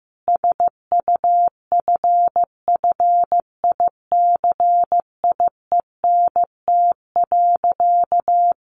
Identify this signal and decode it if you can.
Morse code, heard as a keyed tone, about 15 wpm, 700 Hz. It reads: SUFFICIENT.
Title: Sentences from Top 1400 Words 15wpm